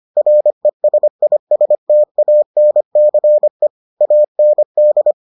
Morse code, heard as a keyed tone, about 25 wpm, 600 Hz. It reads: RESISTANCE AND